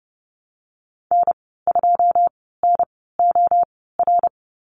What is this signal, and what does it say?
Morse code, heard as a keyed tone, about 30 wpm, spaced as for 17 wpm, 700 Hz. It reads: N2NOR
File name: Lesson 15g - Number 5 - Call Signs 30wpm